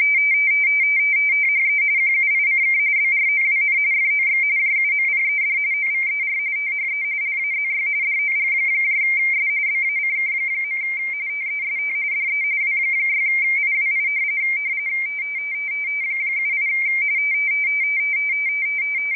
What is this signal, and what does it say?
RTTY, about 45.45 baud. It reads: THE QUICK BROWN FOX JUMPED OVER THE LAZY DOGS BACK. TEST DE W6IWI 78
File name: rtty / websdr recording 14086.0kHz